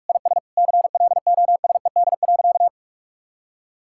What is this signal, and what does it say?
Morse code, heard as a keyed tone, about 43 wpm, 700 Hz. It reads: IS CLOSED.
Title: Sentences from Top 1600 Words 45wpm